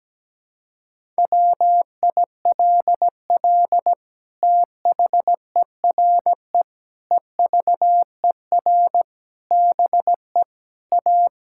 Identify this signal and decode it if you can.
Morse code, heard as a keyed tone, about 17 wpm, 700 Hz. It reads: WILL THERE EVER BE A
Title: Sentences from Top 1000 Words - Review Part 5 - 17wpm